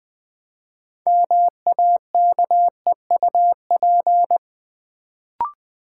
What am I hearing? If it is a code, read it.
Morse code, heard as a keyed tone, about 20 wpm, 700 Hz. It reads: MAKEUP
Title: Build A Word - 2 Letters At A Time - 6-Letter Words - 20wpm